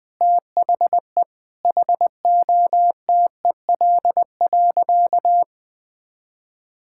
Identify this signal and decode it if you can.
Morse code, heard as a keyed tone, about 20 wpm, 700 Hz. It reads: THE HOTEL.
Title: Sentences from Top 1600 Words 20wpm